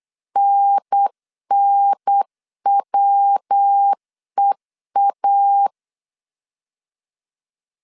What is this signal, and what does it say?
Morse code, heard as a keyed tone, about 8 wpm, 800 Hz. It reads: NNWEA
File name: CW / DET / 1 3 1500C S B 8wpm 4141